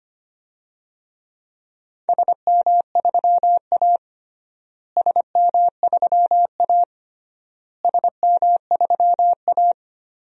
Morse code, heard as a keyed tone, about 25 wpm, 700 Hz. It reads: SM3A SM3A SM3A
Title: DXCC-100 Call Signs - 4 Characters - Repeated 3x - 25wpm